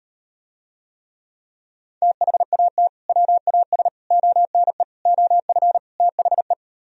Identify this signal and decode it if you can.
Morse code, heard as a keyed tone, about 38 wpm, 700 Hz. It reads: THAT WAS ONE OF THE